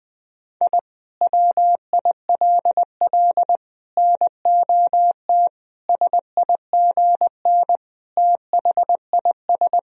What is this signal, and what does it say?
Morse code, heard as a keyed tone, about 20 wpm, 700 Hz. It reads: I WILL NOT SIGN THIS
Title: Sentences from Top 1000 Words - Review Part 2 - 20wpm